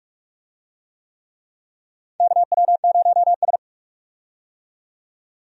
Morse code, heard as a keyed tone, about 43 wpm, 700 Hz. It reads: KW0S
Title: USA Call Signs - 4 Characters - 45wpm